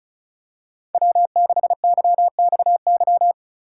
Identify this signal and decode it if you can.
Morse code, heard as a keyed tone, about 36 wpm, 700 Hz. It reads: W6YXY
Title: USA Call Signs - 5 Characters - 2x Character Spacing - 35wpm